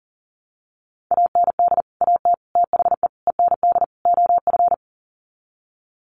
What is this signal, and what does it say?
Morse code, heard as a keyed tone, about 40 wpm, 700 Hz. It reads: AND AT THE END OF
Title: Sets of 5 Words 40wpm